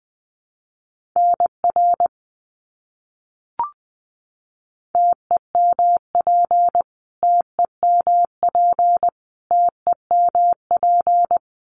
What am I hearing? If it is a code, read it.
Morse code, heard as a keyed tone, about 20 wpm, 700 Hz. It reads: NR TEMP TEMP TEMP
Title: Abbreviations Repeated 3x - 20wpm